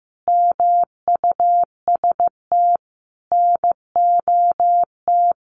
Morse code, heard as a keyed tone, about 15 wpm, 700 Hz. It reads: MUST NOT
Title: Sentences from Top 1100 Words 15wpm